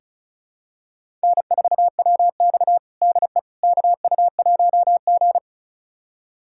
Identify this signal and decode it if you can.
Morse code, heard as a keyed tone, about 34 wpm, 700 Hz. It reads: N4WX DE KU1G
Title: USA Callsign DE Callsign - 4 Characters - 35wpm